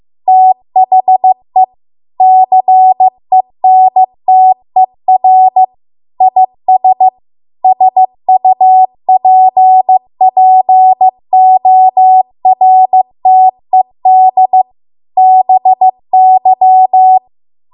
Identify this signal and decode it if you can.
Morse code, heard as a keyed tone, about 15 wpm, 750 Hz. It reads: THE CENTER IS SUPPORTED BY